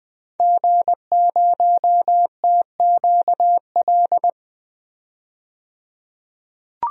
Morse code, heard as a keyed tone, about 20 wpm, 700 Hz. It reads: G0TQL
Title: DXCC-100 Call Signs - 5 Characters - 20wpm